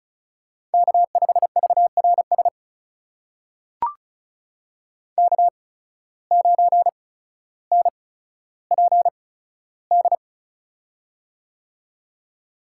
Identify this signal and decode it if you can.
Morse code, heard as a keyed tone, about 36 wpm, spaced as for 15 wpm, 700 Hz. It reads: K5VRS K9NPD
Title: USA Call Signs - 5 Characters - 8x Character Spacing - 35wpm